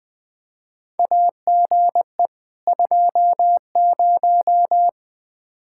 Morse code, heard as a keyed tone, about 20 wpm, 700 Hz. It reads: AGE 20
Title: Practice QSO - Exchange 3 - More Info - 20wpm